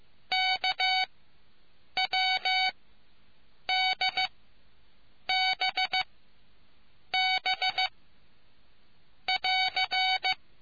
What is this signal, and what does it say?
Morse code, heard as a keyed tone, about 15 wpm, spaced as for 6 wpm, 750 Hz. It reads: KWDBB<AR>